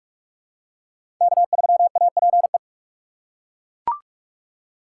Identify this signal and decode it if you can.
Morse code, heard as a keyed tone, about 41 wpm, 700 Hz. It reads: K3APE
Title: USA Call Signs - 5 Characters - 4x Character Spacing - 45wpm